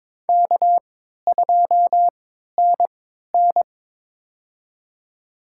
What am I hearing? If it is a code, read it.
Morse code, heard as a keyed tone, about 22 wpm, spaced as for 13 wpm, 700 Hz. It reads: K2NN
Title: USA Call Signs - 4 Characters - 3x Character Spacing - 22wpm